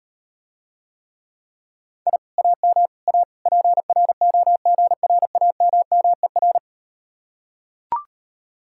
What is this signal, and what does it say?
Morse code, heard as a keyed tone, about 39 wpm, 700 Hz. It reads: I AM A PROGRAMMER